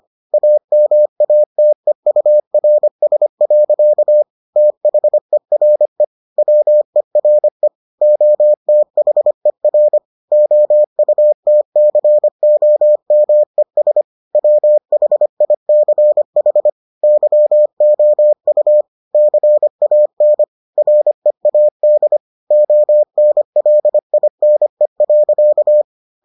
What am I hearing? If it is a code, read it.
Morse code, heard as a keyed tone, about 25 wpm, 600 Hz. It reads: AMATEURS. THERE WERE OTHER OUTCOMES WHICH YOU CAN READ ONLINE.